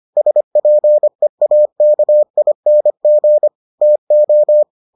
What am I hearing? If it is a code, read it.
Morse code, heard as a keyed tone, about 25 wpm, 600 Hz. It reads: SPEAKING TO